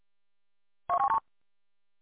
There is no code.